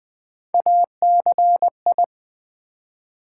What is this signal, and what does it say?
Morse code, heard as a keyed tone, about 20 wpm, 700 Hz. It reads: ACI
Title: Build A Word - 1 Letter At A Time - 4-Letter Words - 20wpm